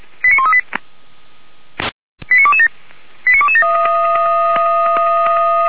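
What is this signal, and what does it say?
Five-tone selective call (ZVEI), heard as quick runs of five tones, tones of about 70 ms. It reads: 89138 89227 89227
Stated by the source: ZVEI five-tone selective call received via SDR, hosted by Feuerwehr Orlamünde.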